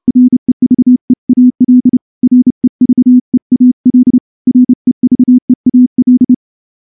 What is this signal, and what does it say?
Morse code, heard as a keyed tone, about 28 wpm, 250 Hz. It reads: REVEAL REVEAL REVEAL
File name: morse-training-30WPM